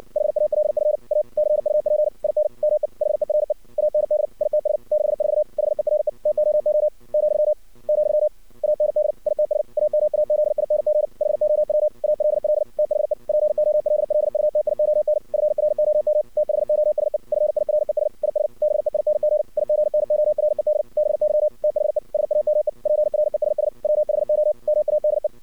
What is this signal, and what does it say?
Morse code, heard as a keyed tone, about 59 wpm, 610 Hz. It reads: HALF A CUP IN ME HERE AND ITS 64 HERE TOO , , AND ITS AMAZING HOW ALL THE POLLUTION FROM THOSE FIRES IS BEING IGNORED BY THE SAME PRESS WHO MADE
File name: archived / CRSnet